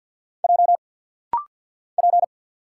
Morse code, heard as a keyed tone, about 49 wpm, 700 Hz. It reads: J P
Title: Single Letter-Number - Rapid-Fire 50wpm